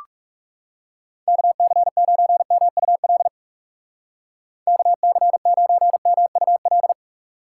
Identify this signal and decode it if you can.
Morse code, heard as a keyed tone, about 44 wpm, 700 Hz. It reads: KC9MUL KC9MUL